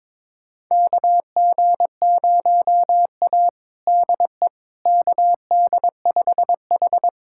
Morse code, heard as a keyed tone, about 22 wpm, 700 Hz. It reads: KG0A DE KD5H
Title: USA Callsign DE Callsign - 4 Characters - 22wpm